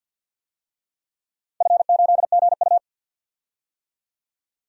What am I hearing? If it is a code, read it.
Morse code, heard as a keyed tone, about 52 wpm, 700 Hz. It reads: F8GU
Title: DXCC-100 Call Signs - 4 Characters - 50wpm